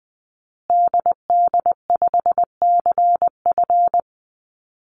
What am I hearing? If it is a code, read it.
Morse code, heard as a keyed tone, about 20 wpm, 700 Hz. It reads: DD5CF
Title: Call Signs Repeated 3x - 20wpm